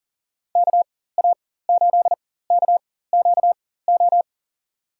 Morse code, heard as a keyed tone, about 38 wpm, spaced as for 18 wpm, 700 Hz. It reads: KA8KQO
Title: USA Call Signs - 6 Characters - 4x Character Spacing - 40wpm